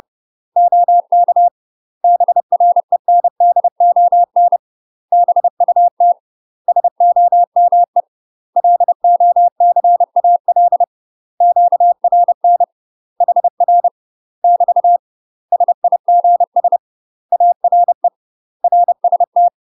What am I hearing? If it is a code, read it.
Morse code, heard as a keyed tone, about 30 wpm, 700 Hz. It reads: OK BRENDON BUT SOME LOCAL QRN HR <BT> SIGS ARE RST
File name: UK MASTER QSOs TRAINING / 30WPM / UK MASTER QSOs 0029